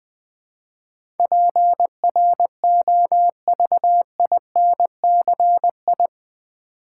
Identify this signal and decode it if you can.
Morse code, heard as a keyed tone, about 20 wpm, 700 Hz. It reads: PROVINCI
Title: Build A Word - 4 Letters At A Time - 10-Letter Words - 20wpm